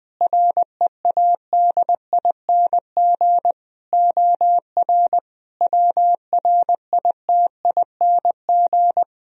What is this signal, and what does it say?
Morse code, heard as a keyed tone, about 20 wpm, 700 Hz. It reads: READING OR WRITING
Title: Sentences from Top 1000 Words - Review Part 3 - 20wpm